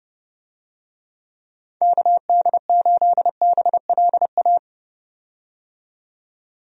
Morse code, heard as a keyed tone, about 32 wpm, 700 Hz. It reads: KD8BLA
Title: USA Call Signs - 6 Characters - 30wpm